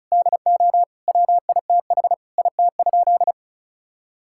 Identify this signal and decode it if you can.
Morse code, heard as a keyed tone, about 36 wpm, 700 Hz. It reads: DO WITH IT?
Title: Sentences from Top 1600 Words 35wpm